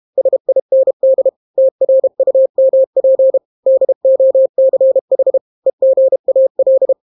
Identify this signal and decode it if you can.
Morse code, heard as a keyed tone, about 32 wpm, 530 Hz. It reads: SIND TRUMP DOCH EGAL